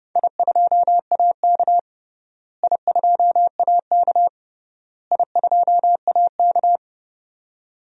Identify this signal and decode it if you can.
Morse code, heard as a keyed tone, about 31 wpm, 700 Hz. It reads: I2AK I2AK I2AK
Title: DXCC-100 Call Signs - 4 Characters - Repeated 3x - 30wpm